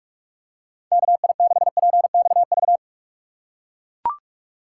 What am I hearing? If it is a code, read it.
Morse code, heard as a keyed tone, about 43 wpm, 700 Hz. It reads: KI6PXV
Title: USA Call Signs - 6 Characters - 3x Character Spacing - 45wpm